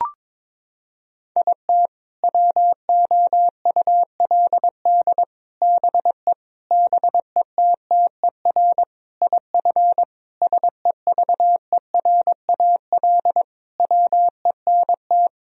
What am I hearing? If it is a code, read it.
Morse code, heard as a keyed tone, about 22 wpm, 700 Hz. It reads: IT WOULD BE BETTER IF SEVERAL WENT
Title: Sentences from Top 1000 Words - Review Part 2 - 22wpm